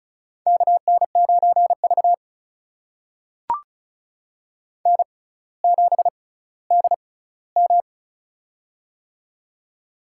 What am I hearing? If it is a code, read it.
Morse code, heard as a keyed tone, about 36 wpm, spaced as for 18 wpm, 700 Hz. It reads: KN9V N7DM
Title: USA Call Signs - 4 Characters - 6x Character Spacing - 35wpm